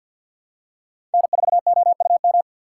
Morse code, heard as a keyed tone, about 47 wpm, 700 Hz. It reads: N4OUM
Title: USA Call Signs - 5 Characters - 4x Character Spacing - 50wpm